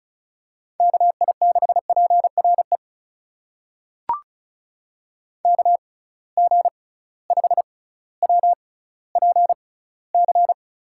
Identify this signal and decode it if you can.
Morse code, heard as a keyed tone, about 36 wpm, spaced as for 18 wpm, 700 Hz. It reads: KI6PRE KG5WPC
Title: USA Call Signs - 6 Characters - 6x Character Spacing - 35wpm